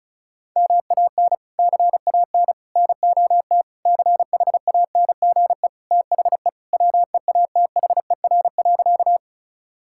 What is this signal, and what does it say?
Morse code, heard as a keyed tone, about 36 wpm, 700 Hz. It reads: MAN CAN NOT CHANGE THE WEATHER.
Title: Sentences from Top 1000 Words - Review Part 5 - 35wpm